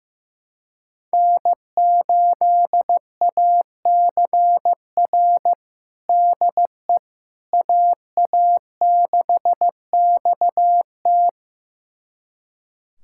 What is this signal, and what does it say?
Morse code, heard as a keyed tone, about 16 wpm, 700 Hz. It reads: N8ACR DE AA6XT
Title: USA Callsign DE Callsign 15wpm